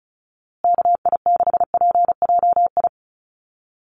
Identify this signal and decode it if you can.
Morse code, heard as a keyed tone, about 34 wpm, 700 Hz. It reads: KI6PJI
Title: USA Call Signs Repeated 3x 35wpm